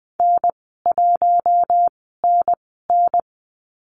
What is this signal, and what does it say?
Morse code, heard as a keyed tone, about 20 wpm, spaced as for 14 wpm, 700 Hz. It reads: N1NN
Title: 069 Lesson 12g - Letter d - Call Signs 20wpm